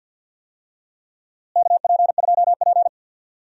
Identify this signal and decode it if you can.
Morse code, heard as a keyed tone, about 48 wpm, 700 Hz. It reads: KP2P
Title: DXCC-100 Call Signs - 4 Characters - Repeated 2x - 50wpm